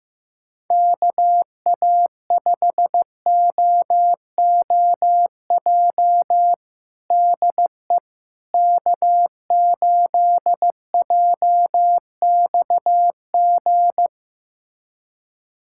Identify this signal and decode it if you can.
Morse code, heard as a keyed tone, about 15 wpm, 700 Hz. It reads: KA5OOJ DE K8JXG
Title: USA Callsign DE Callsign 15wpm